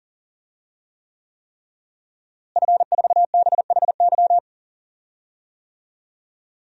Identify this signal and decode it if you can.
Morse code, heard as a keyed tone, about 37 wpm, 700 Hz. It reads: F4BHY
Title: DXCC-100 Call Signs - 5 Characters - Repeated 3x - 40wpm